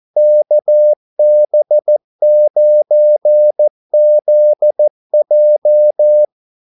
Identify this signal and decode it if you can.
Morse code, heard as a keyed tone, about 14 wpm, 600 Hz. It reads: KB9ZJ
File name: Callsigns-14wpm